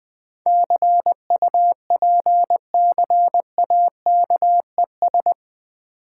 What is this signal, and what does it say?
Morse code, heard as a keyed tone, about 21 wpm, 690 Hz. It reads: CUPCAKES
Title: Difficult Words with Hints - 20wpm